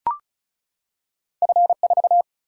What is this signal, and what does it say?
Morse code, heard as a keyed tone, about 36 wpm, 700 Hz. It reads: F 4
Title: Single-Letter Single-Number 35wpm